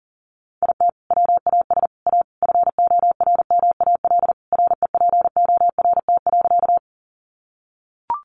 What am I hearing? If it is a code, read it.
Morse code, heard as a keyed tone, about 40 wpm, 700 Hz. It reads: IT WAS A FORMAL REPORT.
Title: Sentences from Top 1300 Words 40wpm